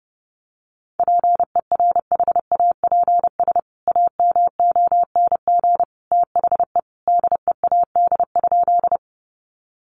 Morse code, heard as a keyed tone, about 30 wpm, 700 Hz. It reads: PERHAPS AMONG THE DEAD?
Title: Sentences from Top 1500 Words 30wpm